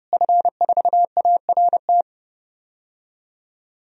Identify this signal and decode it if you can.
Morse code, heard as a keyed tone, about 30 wpm, 700 Hz. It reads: F4ART
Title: DXCC-100 Call Signs - 5 Characters - Repeated 2x - 30wpm